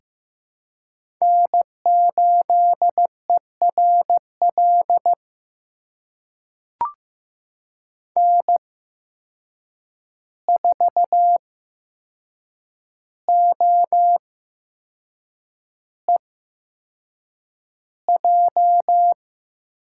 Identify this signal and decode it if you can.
Morse code, heard as a keyed tone, about 15 wpm, spaced as for 6 wpm, 700 Hz. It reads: N8ERL N4OEJ
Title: USA Call Signs - 5 Characters - 8x Character Spacing - 15wpm